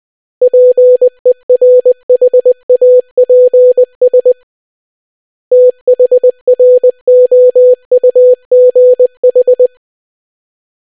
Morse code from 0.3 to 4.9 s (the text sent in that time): PERHAPS